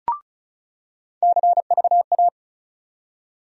Morse code, heard as a keyed tone, about 36 wpm, 700 Hz. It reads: CVA
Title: USA Call Sign Postfixes - 3 Letters 35wpm